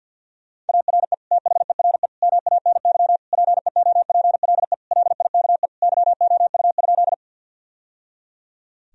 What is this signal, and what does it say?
Morse code, heard as a keyed tone, about 52 wpm, 700 Hz. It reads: ARE THERE MANY PEOPLE LIKE YOU?
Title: Sentences from Top 1000 Words - Review Part 1 - 50wpm